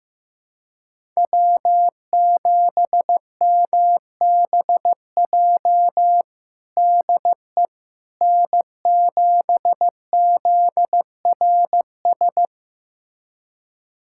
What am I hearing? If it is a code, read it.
Morse code, heard as a keyed tone, about 15 wpm, 700 Hz. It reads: W7MBJ DE N7ZRS
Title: USA Callsign DE Callsign - 5 Characters - 15wpm